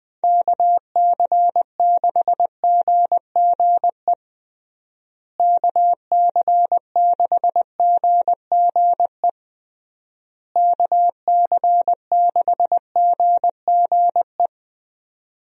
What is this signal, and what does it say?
Morse code, heard as a keyed tone, about 20 wpm, 700 Hz. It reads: KC6GGE KC6GGE KC6GGE
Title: USA Call Signs - 6 Characters - Repeated 3x - 20wpm